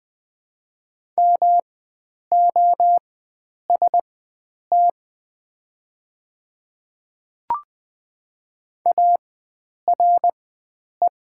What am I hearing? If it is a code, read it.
Morse code, heard as a keyed tone, about 20 wpm, spaced as for 9 wpm, 700 Hz. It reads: MOST ARE
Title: 4x Character Spacing - Top 100 Words - 20wpm